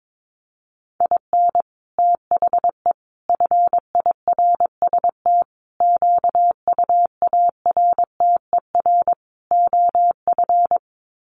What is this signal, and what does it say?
Morse code, heard as a keyed tone, about 22 wpm, 700 Hz. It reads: IN THE FIRST QUARTER OF
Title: Sets of 5 Words Encore2 22wpm